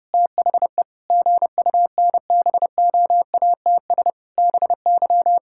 Morse code, heard as a keyed tone, about 32 wpm, 700 Hz. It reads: THE GUNBOATS BY
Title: Pangram Challenge v2 - 30wpm